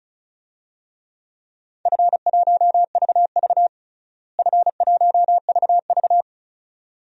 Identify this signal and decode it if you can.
Morse code, heard as a keyed tone, about 36 wpm, 700 Hz. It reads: F1VV F1VV
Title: DXCC-100 Call Signs - 4 Characters - Repeated 2x - 35wpm